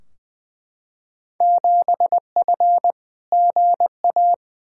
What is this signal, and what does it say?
Morse code, heard as a keyed tone, about 21 wpm, 700 Hz. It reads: 7F GA
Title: Field Day Exchanges Encore - 20wpm